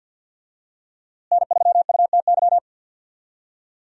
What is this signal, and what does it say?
Morse code, heard as a keyed tone, about 46 wpm, 700 Hz. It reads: N3UTY
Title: USA Call Signs - 5 Characters - Repeated 2x - 50wpm